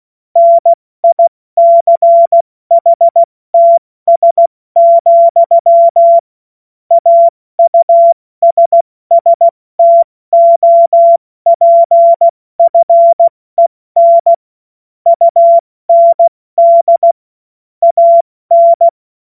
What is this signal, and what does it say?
Morse code, heard as a keyed tone, about 16 wpm, 680 Hz. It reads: NICHTS, AUSSTOPFEN UND AN
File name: Morsekurs / Witze / Witz 28. September 2021, 70 F80